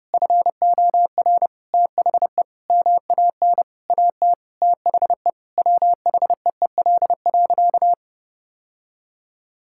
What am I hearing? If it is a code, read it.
Morse code, heard as a keyed tone, about 30 wpm, 700 Hz. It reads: FOR THE MAN AT THE WHEEL.
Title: Sentences from Top 1000 Words - Review Part 5 - 30wpm